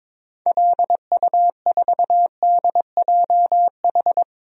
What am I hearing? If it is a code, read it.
Morse code, heard as a keyed tone, about 22 wpm, 700 Hz. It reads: LU4DJH
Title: DXCC-100 Call Signs - 6 Characters - Repeated 2x - 22wpm